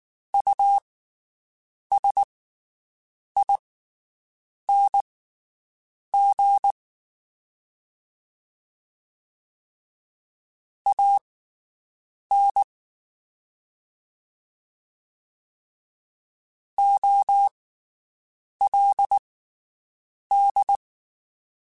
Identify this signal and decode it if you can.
Morse code, heard as a keyed tone, about 19 wpm, spaced as for 7 wpm, 800 Hz. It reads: USING AN OLD
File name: QSOs Spaced